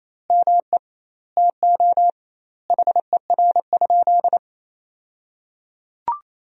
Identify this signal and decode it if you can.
Morse code, heard as a keyed tone, about 28 wpm, 700 Hz. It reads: ME TO HER?
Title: 2x Word-Spacing - Sentences from Top 200 Words - 28wpm